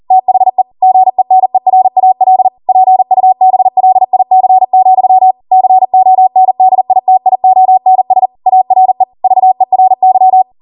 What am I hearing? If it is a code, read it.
Morse code, heard as a keyed tone, about 40 wpm, 750 Hz. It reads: THE GENERAL PUBLIC, CONDITIONS ARE VERY